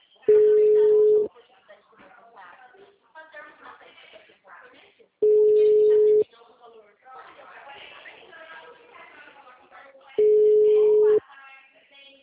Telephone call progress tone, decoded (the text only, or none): ringback tone